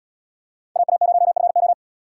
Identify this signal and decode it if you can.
Morse code, heard as a keyed tone, about 74 wpm, 700 Hz. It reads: AA0FO